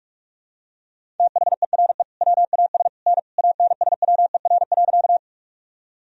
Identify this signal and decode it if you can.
Morse code, heard as a keyed tone, about 46 wpm, 700 Hz. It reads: THERE WAS N ANSWER.
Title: Sentences from Top 300 Words and Abbreviations 45wpm